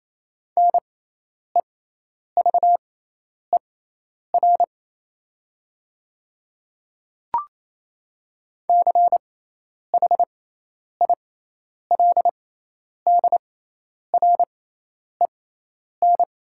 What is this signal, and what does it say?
Morse code, heard as a keyed tone, about 28 wpm, spaced as for 10 wpm, 700 Hz. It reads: NEVER CHILDREN
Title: 6x Character Spacing - Top 500 Words - 28wpm